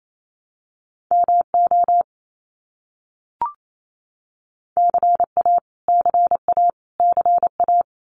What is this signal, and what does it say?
Morse code, heard as a keyed tone, about 28 wpm, 700 Hz. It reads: MO CA CA CA
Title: US States Repeated 3x 28wpm